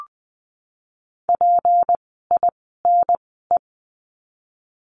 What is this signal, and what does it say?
Morse code, heard as a keyed tone, about 20 wpm, spaced as for 15 wpm, 700 Hz. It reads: PINE
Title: 153 Lesson 25f - Letter p - Words 20wpm